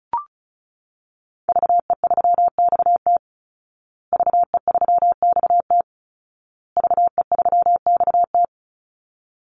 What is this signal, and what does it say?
Morse code, heard as a keyed tone, about 34 wpm, 700 Hz. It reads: VE3XT VE3XT VE3XT
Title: Call Signs Repeated 2x - 35wpm